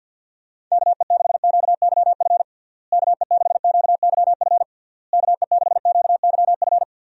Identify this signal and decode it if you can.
Morse code, heard as a keyed tone, about 50 wpm, 700 Hz. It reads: KE6QYF KE6QYF KE6QYF
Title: USA Call Signs - 6 Characters - Repeated 3x - 50wpm